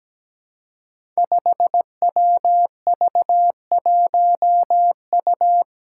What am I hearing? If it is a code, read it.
Morse code, heard as a keyed tone, about 17 wpm, 700 Hz. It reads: 5WV1U
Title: Five Challenging Letter-Number Combinations 17wpm